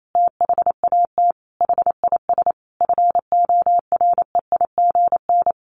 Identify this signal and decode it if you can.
Morse code, heard as a keyed tone, about 28 wpm, 700 Hz. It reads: THAT HIS FOREIGN
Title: Sentences from Top 1400 Words 28wpm